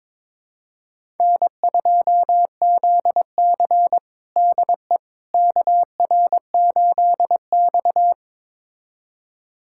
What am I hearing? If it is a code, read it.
Morse code, heard as a keyed tone, about 22 wpm, 700 Hz. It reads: N2ZC DE KR8X
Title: USA Callsign DE Callsign - 4 Characters - 22wpm